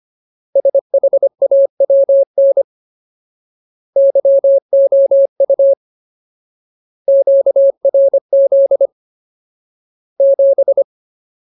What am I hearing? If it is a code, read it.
Morse code, heard as a keyed tone, about 25 wpm, 550 Hz. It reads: SHAWN YOU QRZ 7
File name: Ultimate Word List / 25WPM Ultimate